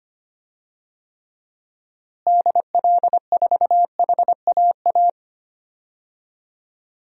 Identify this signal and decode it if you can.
Morse code, heard as a keyed tone, about 25 wpm, 700 Hz. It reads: DL4HAA
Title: DXCC-100 Call Signs - 6 Characters - Repeated 3x - 25wpm